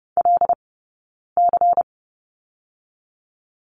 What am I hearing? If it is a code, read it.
Morse code, heard as a keyed tone, about 30 wpm, 700 Hz. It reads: L C
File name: Lesson 19d - Letter w - Two Characters 30wpm